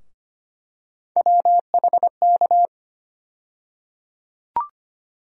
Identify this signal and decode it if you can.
Morse code, heard as a keyed tone, about 25 wpm, 700 Hz. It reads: WHK
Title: USA Call Sign Postfixes - 3 Letters 25wpm